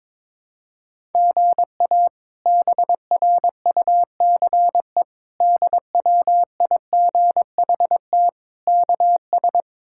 Morse code, heard as a keyed tone, about 22 wpm, 700 Hz. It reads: GA BRUCE DWIGHT KS
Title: K1USN SST QSO Practice - 22wpm